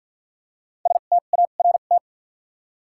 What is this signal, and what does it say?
Morse code, heard as a keyed tone, about 53 wpm, spaced as for 37 wpm, 700 Hz. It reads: START
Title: 2x Character Spacing - Top 300 Words - 50wpm